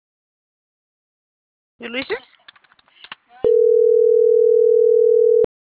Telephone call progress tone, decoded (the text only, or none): ringback tone